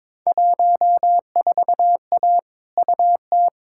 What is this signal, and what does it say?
Morse code, heard as a keyed tone, about 22 wpm, 700 Hz. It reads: 14A UT
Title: Field Day Exchanges Encore - 22wpm